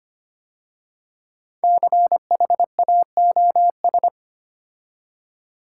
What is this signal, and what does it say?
Morse code, heard as a keyed tone, about 25 wpm, 700 Hz. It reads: CHAOS
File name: 5 Letter Words - First 3 Letters Given - 25wpm